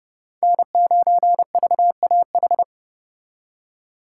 Morse code, heard as a keyed tone, about 39 wpm, 700 Hz. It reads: N9VAH